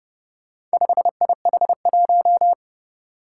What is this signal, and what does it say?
Morse code, heard as a keyed tone, about 30 wpm, 700 Hz. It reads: 5IH1
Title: Four Challenging Letter-Number Combinations 30wpm